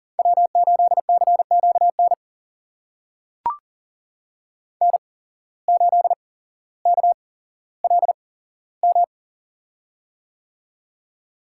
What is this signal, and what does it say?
Morse code, heard as a keyed tone, about 40 wpm, spaced as for 17 wpm, 700 Hz. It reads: W8CQN N8KLM
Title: USA Call Signs - 5 Characters - 8x Character Spacing - 40wpm